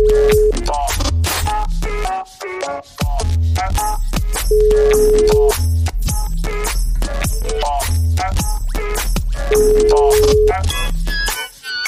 Ringback tone (Europe, 425 Hz).